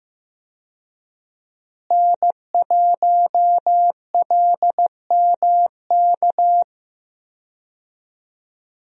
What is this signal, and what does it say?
Morse code, heard as a keyed tone, about 15 wpm, 700 Hz. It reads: N1LMK
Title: USA Call Signs - 5 Characters - Repeated 2x - 15wpm